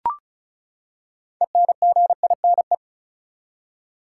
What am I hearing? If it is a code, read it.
Morse code, heard as a keyed tone, about 34 wpm, 700 Hz. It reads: ENGINE
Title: Six Letter Words - 35wpm